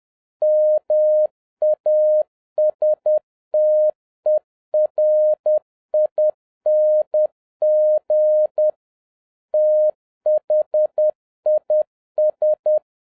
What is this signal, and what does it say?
Morse code, heard as a keyed tone, about 10 wpm, 610 Hz. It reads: MASTERING THIS
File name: 2011-07-mp3 / n21 GREETINGS FROM SINGAPORE 10wpm